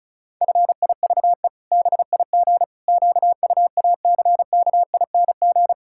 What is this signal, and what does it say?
Morse code, heard as a keyed tone, about 34 wpm, 700 Hz. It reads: FIVE BIG QUACKING
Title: Pangram Challenge v2 - 35wpm